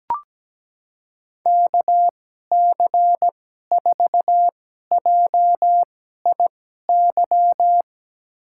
Morse code, heard as a keyed tone, about 17 wpm, spaced as for 12 wpm, 700 Hz. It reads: KC4JIY
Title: USA Call Signs - 6 Characters - 2x Character Spacing - 17wpm